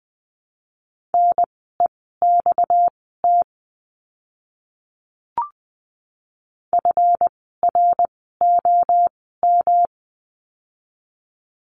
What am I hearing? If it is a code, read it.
Morse code, heard as a keyed tone, about 20 wpm, spaced as for 14 wpm, 700 Hz. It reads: NEXT FROM
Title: 237 Lesson 38f - Letter q - Words 20wpm